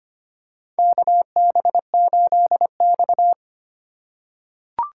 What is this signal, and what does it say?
Morse code, heard as a keyed tone, about 25 wpm, 700 Hz. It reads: KB8X
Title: USA Call Signs - 4 Characters - 3x Character Spacing - 25wpm